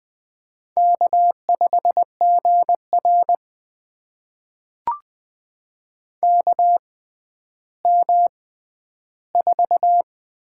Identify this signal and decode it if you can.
Morse code, heard as a keyed tone, about 20 wpm, spaced as for 11 wpm, 700 Hz. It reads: K5GR KM4
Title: USA Call Signs - 6x Character Spacing - 20wpm